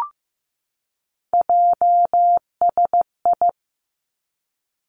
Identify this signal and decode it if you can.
Morse code, heard as a keyed tone, about 15 wpm, 700 Hz. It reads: JSI
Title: Three Challenging Letter-Number Combinations 15wpm